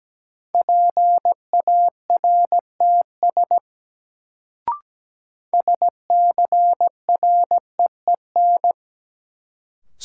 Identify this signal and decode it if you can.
Morse code, heard as a keyed tone, about 17 wpm, 700 Hz. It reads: PARTS SCREEN
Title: Sentences from Top 1000 Words - Review Part 6 - New Word Drill - 17wpm